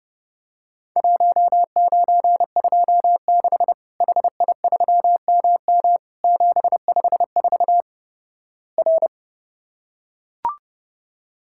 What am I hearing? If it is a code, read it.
Morse code, heard as a keyed tone, about 30 wpm, 700 Hz. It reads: 1926 HI3MM 754 R
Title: Worked All Europe DX Contest - QTC Exchange - 30wpm